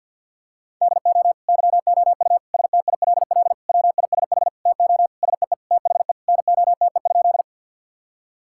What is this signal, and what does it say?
Morse code, heard as a keyed tone, about 50 wpm, 700 Hz. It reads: DO YOU STILL WISH TO SEE THE NOTE?
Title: Sentences from Top 1600 Words 50wpm